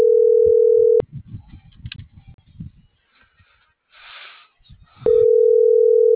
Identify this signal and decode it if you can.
Ringback tone (North America), 440 + 480 Hz.